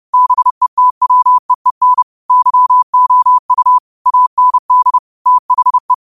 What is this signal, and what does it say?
Morse code, heard as a keyed tone, about 30 wpm, 1000 Hz. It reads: BETWEEN YOU AND THE
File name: qotdfiles / 30wpm